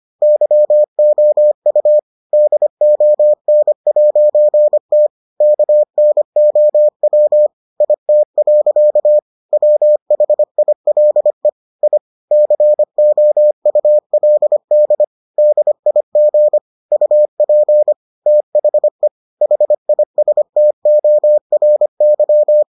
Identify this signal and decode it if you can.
Morse code, heard as a keyed tone, about 25 wpm, 600 Hz. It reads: YOU DON'T KNOW IT. WHILE I COULD DIG UP THE HISTORY